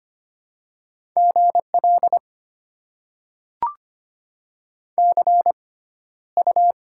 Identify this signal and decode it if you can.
Morse code, heard as a keyed tone, about 25 wpm, spaced as for 13 wpm, 690 Hz. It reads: GL CU
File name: Top 90 QSO Elements - 6x Character Spacing - 25wpm